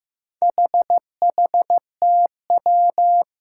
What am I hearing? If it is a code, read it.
Morse code, heard as a keyed tone, about 15 wpm, 700 Hz. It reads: HHTW
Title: Four Challenging Letter-Number Combinations 15wpm